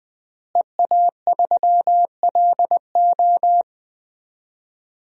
Morse code, heard as a keyed tone, about 20 wpm, 700 Hz. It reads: EA3LO
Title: DXCC-100 Call Signs - 5 Characters - Repeated 3x - 20wpm